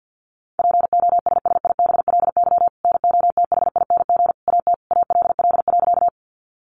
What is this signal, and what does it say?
Morse code, heard as a keyed tone, about 50 wpm, 700 Hz. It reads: POSSIBLY NOTHING AT ALL.
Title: Sentences from Top 1300 Words 50wpm